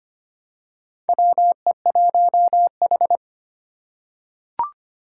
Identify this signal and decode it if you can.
Morse code, heard as a keyed tone, about 25 wpm, 700 Hz. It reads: WE1H
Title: Four Challenging Letter-Number Combinations 25wpm